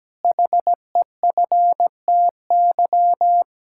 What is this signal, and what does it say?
Morse code, heard as a keyed tone, about 17 wpm, 700 Hz. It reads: HEFTY
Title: Pangram Challenge v2 - 17wpm